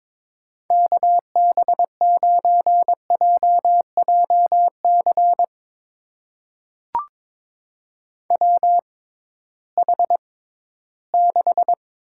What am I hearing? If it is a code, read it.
Morse code, heard as a keyed tone, about 22 wpm, spaced as for 14 wpm, 700 Hz. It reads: KB9JJC WH6
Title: USA Call Signs - 6x Character Spacing - 22wpm